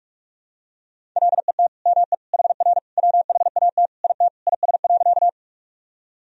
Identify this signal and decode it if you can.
Morse code, heard as a keyed tone, about 45 wpm, 700 Hz. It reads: LET ME HR WHAT IT IS.
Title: Sentences from Top 300 Words and Abbreviations 45wpm